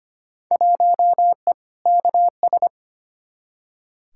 Morse code, heard as a keyed tone, about 25 wpm, 700 Hz. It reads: 1E KS
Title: Field Day Exchanges - 25wpm